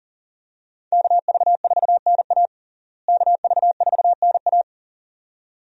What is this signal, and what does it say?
Morse code, heard as a keyed tone, about 40 wpm, 700 Hz. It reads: KV4NA KV4NA
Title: USA Call Signs - 5 Characters - Repeated 2x - 40wpm